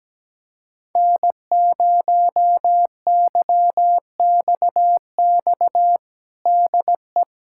Morse code, heard as a keyed tone, about 17 wpm, 700 Hz. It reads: N0YXX DE
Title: USA Callsign DE Callsign - 5 Characters - 17wpm